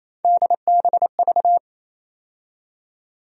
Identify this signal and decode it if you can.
Morse code, heard as a keyed tone, about 28 wpm, 700 Hz. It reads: DBV
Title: USA Call Sign Postfixes - 3 Letters 28wpm